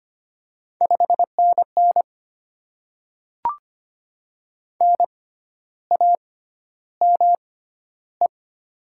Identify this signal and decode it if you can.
Morse code, heard as a keyed tone, about 25 wpm, spaced as for 12 wpm, 700 Hz. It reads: 5NN NAME
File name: Top 20 QSO Elements - 6x Character Spacing - 25wpm